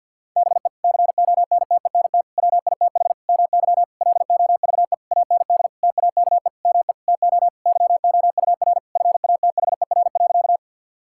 Morse code, heard as a keyed tone, about 52 wpm, 700 Hz. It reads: BE CONTENT WITH MY LOVE AND TAKE ME TO YOUR FATHER.
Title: Sentences from Top 1000 Words - Review Part 6 - 50wpm